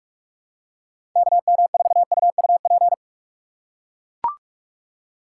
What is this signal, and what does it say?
Morse code, heard as a keyed tone, about 41 wpm, 700 Hz. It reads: KM4UUP
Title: USA Call Signs - 3x Character Spacing - 45wpm